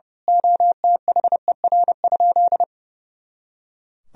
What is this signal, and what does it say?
Morse code, heard as a keyed tone, about 31 wpm, 700 Hz. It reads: OTHER?
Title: Sentences from Top 200 Words and Abbreviations 30wpm